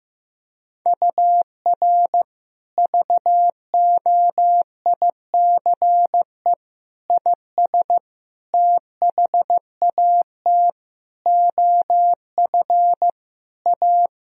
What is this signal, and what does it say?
Morse code, heard as a keyed tone, about 15 wpm, 700 Hz. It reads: UR VOICE IS THAT OF A
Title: Sentences from Top 500 Words and Abbreviations 15wpm